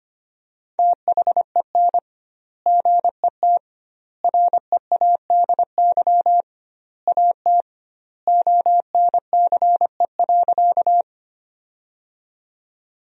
Morse code, heard as a keyed tone, about 25 wpm, 700 Hz. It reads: THEN GET READY AT ONCE.
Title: 2x Word-Spacing - Sentences from Top 400 Words - 25wpm